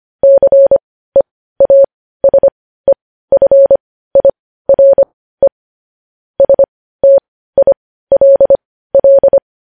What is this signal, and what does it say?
Morse code, heard as a keyed tone, about 25 wpm, spaced as for 15 wpm, 550 Hz. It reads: CEASEFIRE STILL
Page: news-headlines-in-morse-code-at-15-wpm